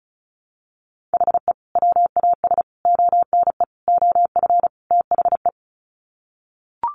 Morse code, heard as a keyed tone, about 34 wpm, 700 Hz. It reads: HE WAS ONE OF THE
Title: Sets of 5 Words 35wpm